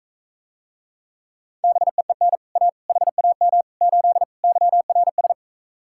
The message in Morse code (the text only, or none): BEEN A HAM 8 YRS